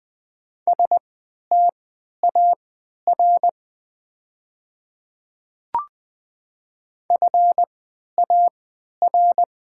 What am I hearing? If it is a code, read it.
Morse code, heard as a keyed tone, about 20 wpm, spaced as for 11 wpm, 690 Hz. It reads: STAR FAR
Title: 3x Character Spacing - Top 500 Words - 20wpm